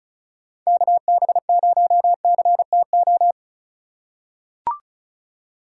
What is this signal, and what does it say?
Morse code, heard as a keyed tone, about 36 wpm, 700 Hz. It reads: KB0CTO
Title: USA Call Signs - 6 Characters - 4x Character Spacing - 35wpm